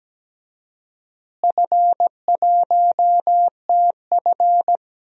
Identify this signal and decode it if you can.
Morse code, heard as a keyed tone, about 17 wpm, 700 Hz. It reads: F1TF
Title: DXCC-100 Call Signs - 4 Characters - Repeated 2x - 17wpm